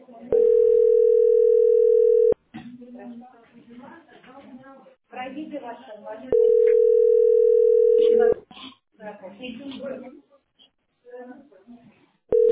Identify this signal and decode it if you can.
Ringback tone (North America), 440 + 480 Hz, with speech.
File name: ce6f425e-19e6-43f5-a331